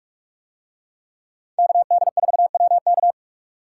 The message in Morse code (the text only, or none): KD4WK